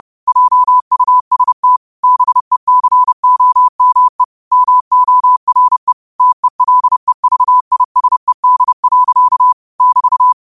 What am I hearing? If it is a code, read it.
Morse code, heard as a keyed tone, about 30 wpm, 1000 Hz. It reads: JUST BECOME MORE TELEVISED. <BT>